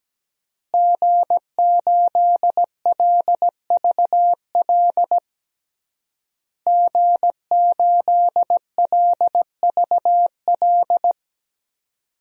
Morse code, heard as a keyed tone, about 17 wpm, 700 Hz. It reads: G8LVL G8LVL
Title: DXCC-100 Call Signs - 5 Characters - Repeated 3x - 17wpm